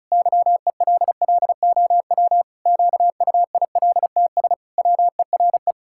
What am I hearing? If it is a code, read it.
Morse code, heard as a keyed tone, about 36 wpm, 700 Hz. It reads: YELLOW QUILTS WERE
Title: Pangram Challenge v2 - 35wpm